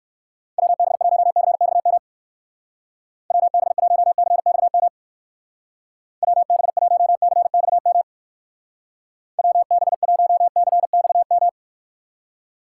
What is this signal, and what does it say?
Morse code, heard as a keyed tone, about 56 wpm, 700 Hz. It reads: WB1CXM WB1CXM WB1CXM WB1CXM